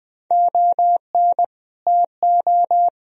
Morse code, heard as a keyed tone, about 21 wpm, 700 Hz. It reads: ON TO
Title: Sentences from Top 1000 Words - Review Part 2 - 20wpm